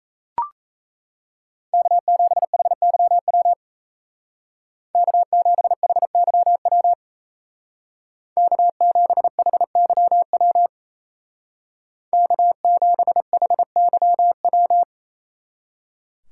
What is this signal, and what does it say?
Morse code, heard as a keyed tone, about 33 wpm, 700 Hz. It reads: K7HYW K7HYW K7HYW K7HYW
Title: Speed Racing - USA Call Signs 28wpm